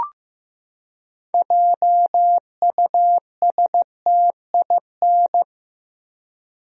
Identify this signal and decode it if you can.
Morse code, heard as a keyed tone, about 15 wpm, 700 Hz. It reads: JUSTIN
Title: USA Common Names - 15wpm